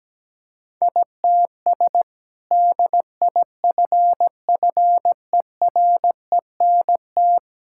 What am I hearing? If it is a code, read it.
Morse code, heard as a keyed tone, about 17 wpm, 700 Hz. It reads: ITS DIFFERENT
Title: Sentences from Top 1000 Words - Review Part 5 - 17wpm